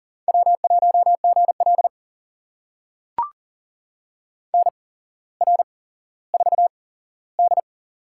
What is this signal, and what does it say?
Morse code, heard as a keyed tone, about 42 wpm, spaced as for 17 wpm, 700 Hz. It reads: W1GL NR4D
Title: USA Call Signs - 4 Characters - 8x Character Spacing - 40wpm